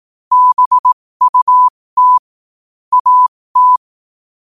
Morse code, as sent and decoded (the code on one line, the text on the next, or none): -... ..- - / .- -
BUT AT